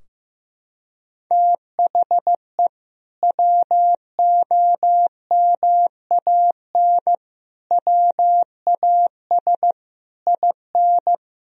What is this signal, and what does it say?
Morse code, heard as a keyed tone, about 15 wpm, 700 Hz. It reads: THE WOMAN WAS IN
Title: Sentences from Top 1000 Words - Review Part 3 - 15wpm